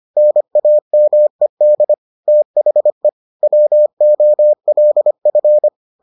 Morse code, heard as a keyed tone, about 25 wpm, 600 Hz. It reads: NAMED THE WOLF